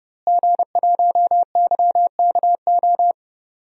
Morse code, heard as a keyed tone, about 30 wpm, 700 Hz. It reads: G1YKO
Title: DXCC-100 Call Signs - 5 Characters - Repeated 3x - 30wpm